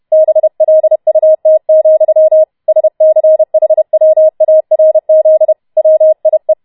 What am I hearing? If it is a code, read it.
Morse code, heard as a keyed tone, about 31 wpm, 630 Hz. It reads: BLUT, SCHWARZ WIE